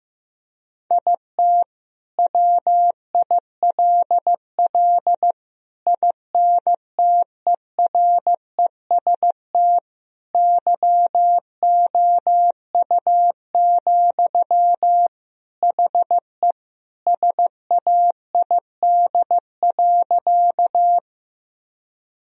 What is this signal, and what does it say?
Morse code, heard as a keyed tone, about 15 wpm, 700 Hz. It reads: IT WILL INTEREST YOU, HE SAID.
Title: Sentences from Top 1000 Words - Review Part 2 - 15wpm